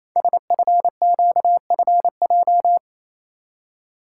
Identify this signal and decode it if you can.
Morse code, heard as a keyed tone, about 27 wpm, 700 Hz. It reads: SFQFJ
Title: Code Groups - 5 Letters - 28wpm